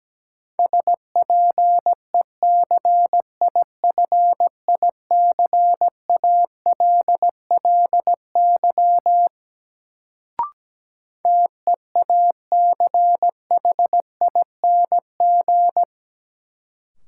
Morse code, heard as a keyed tone, about 17 wpm, 700 Hz. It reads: SPECIFICALLY TEACHING
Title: Sentences from Top 1100 Words - New Word Drill Alt - 17wpm